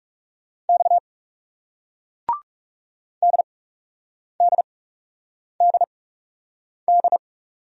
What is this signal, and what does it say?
Morse code, heard as a keyed tone, about 40 wpm, spaced as for 8 wpm, 700 Hz. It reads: X DDDD